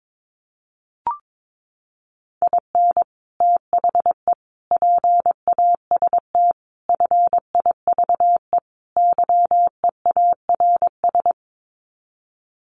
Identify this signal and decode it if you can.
Morse code, heard as a keyed tone, about 22 wpm, 700 Hz. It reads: IN THE PAST FIVE YEARS
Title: Sets of 5 Words Encore 22wpm